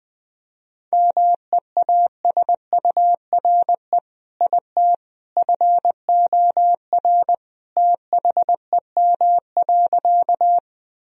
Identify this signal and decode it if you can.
Morse code, heard as a keyed tone, about 20 wpm, 700 Hz. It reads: MEASURE IT FOR THEM.
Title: Sentences from Top 1000 Words - Review Part 2 - 20wpm